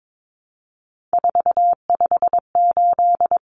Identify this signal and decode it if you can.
Morse code, heard as a keyed tone, about 22 wpm, 700 Hz. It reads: 458
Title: Numbers - 3-Digit 22wpm